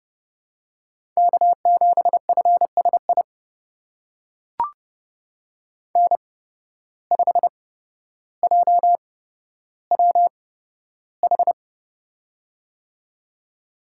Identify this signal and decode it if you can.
Morse code, heard as a keyed tone, about 30 wpm, spaced as for 13 wpm, 700 Hz. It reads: K7FSI N5JWH
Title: USA Call Signs - 5 Characters - 8x Character Spacing - 30wpm